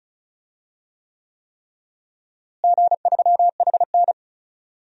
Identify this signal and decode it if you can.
Morse code, heard as a keyed tone, about 36 wpm, 700 Hz. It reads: G3HN
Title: DXCC-100 Call Signs - 4 Characters - Repeated 3x - 35wpm